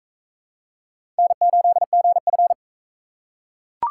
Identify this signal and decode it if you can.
Morse code, heard as a keyed tone, about 39 wpm, 700 Hz. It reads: N8GF